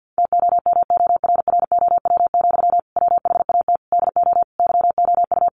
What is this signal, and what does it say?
Morse code, heard as a keyed tone, about 50 wpm, 700 Hz. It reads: TOMORROW, WHAT DO YOU
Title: Sentences from Top 1500 Words 50wpm